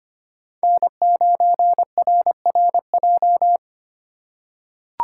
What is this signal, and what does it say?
Morse code, heard as a keyed tone, about 25 wpm, 700 Hz. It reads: N9RRJ
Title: USA Call Signs - 5 Characters - 3x Character Spacing - 25wpm